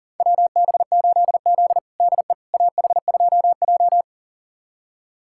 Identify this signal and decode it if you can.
Morse code, heard as a keyed tone, about 40 wpm, 700 Hz. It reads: WB8Z DE AH2J
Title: USA Callsign DE Callsign - 4 Characters - 40wpm